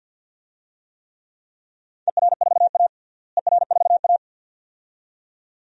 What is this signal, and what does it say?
Morse code, heard as a keyed tone, about 51 wpm, 700 Hz. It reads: ER4A ER4A
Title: DXCC-100 Call Signs - 4 Characters - Repeated 2x - 50wpm